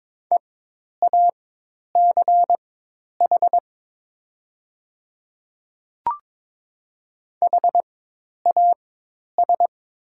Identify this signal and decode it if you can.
Morse code, heard as a keyed tone, about 22 wpm, spaced as for 10 wpm, 700 Hz. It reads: EACH HAS
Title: 4x Character Spacing - Top 100 Words - 22wpm